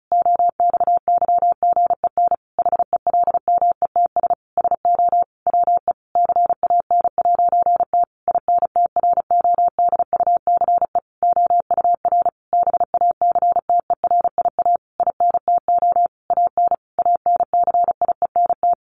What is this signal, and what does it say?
Morse code, heard as a keyed tone, about 34 wpm, 700 Hz. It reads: OXYGEN HELMETS SO WE CAN'T INTRODUCE OUR BACTERIA INTO AN ANCIENT
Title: A Sound Of Thunder 35wpm